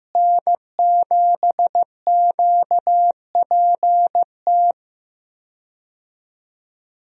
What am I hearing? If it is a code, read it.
Morse code, heard as a keyed tone, about 15 wpm, 700 Hz. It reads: N7QPT
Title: USA Call Signs - 5 Characters - Repeated 3x - 15wpm